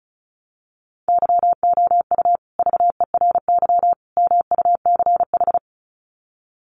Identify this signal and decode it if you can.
Morse code, heard as a keyed tone, about 34 wpm, 700 Hz. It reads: YOU VERY MUCH
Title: Sets of 3 Words 35wpm